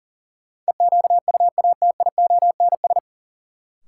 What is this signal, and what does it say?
Morse code, heard as a keyed tone, about 40 wpm, 700 Hz. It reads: EQUATIONS
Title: Sets of 5 Words - New Word Drill - 40wpm